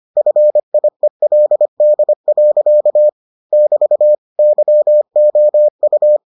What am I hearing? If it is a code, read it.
Morse code, heard as a keyed tone, about 25 wpm, 600 Hz. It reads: FIELD. <BT> YOU